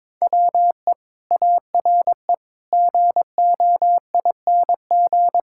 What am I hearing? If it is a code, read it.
Morse code, heard as a keyed tone, about 22 wpm, 700 Hz. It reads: WE ARE GOING
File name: practice / 22wpm / Sentences with 4-Word Phrases - 22wpm